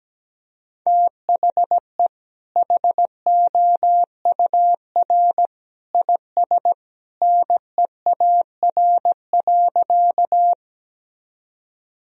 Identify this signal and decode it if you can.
Morse code, heard as a keyed tone, about 17 wpm, 700 Hz. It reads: THE HOUR IS NEAR.
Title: Sentences from Top 1000 Words - Review Part 1 - 17wpm